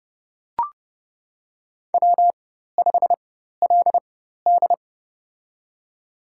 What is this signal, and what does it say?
Morse code, heard as a keyed tone, about 30 wpm, spaced as for 14 wpm, 700 Hz. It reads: W5LD
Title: USA Call Signs - 4 Characters - 4x Character Spacing - 30wpm